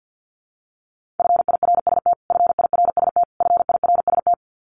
Morse code, heard as a keyed tone, about 50 wpm, 700 Hz. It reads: FIRST FIRST FIRST
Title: Top 500 Words Repeated 50wpm